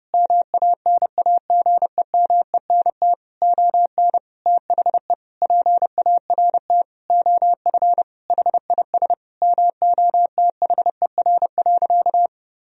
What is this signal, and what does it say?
Morse code, heard as a keyed tone, about 30 wpm, 700 Hz. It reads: MANAGEMENT ON THE PART OF HIS MOTHER.
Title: Sentences from Top 1000 Words - Review Part 4 - 30wpm